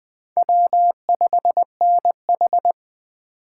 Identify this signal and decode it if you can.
Morse code, heard as a keyed tone, about 19 wpm, 700 Hz. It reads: W5NH